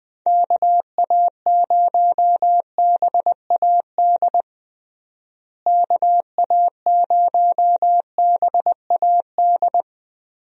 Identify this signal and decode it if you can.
Morse code, heard as a keyed tone, about 20 wpm, 700 Hz. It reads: KA0BAD KA0BAD
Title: USA Call Signs - 6 Characters - Repeated 2x - 20wpm